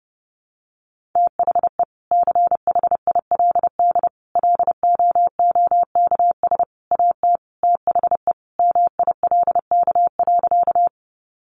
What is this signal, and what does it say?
Morse code, heard as a keyed tone, about 30 wpm, 700 Hz. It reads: THE CHILD LOOKS AT THE MILK.
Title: Additional Easy Phrases 30wpm